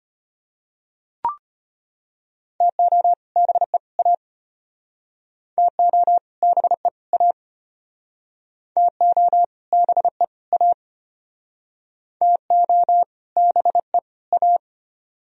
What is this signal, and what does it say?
Morse code, heard as a keyed tone, about 32 wpm, 700 Hz. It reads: TO BE A TO BE A TO BE A TO BE A